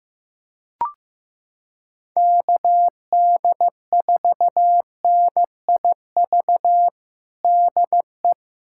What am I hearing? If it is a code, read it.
Morse code, heard as a keyed tone, about 15 wpm, 700 Hz. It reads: KD4NIV DE
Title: USA Callsign DE Callsign 15wpm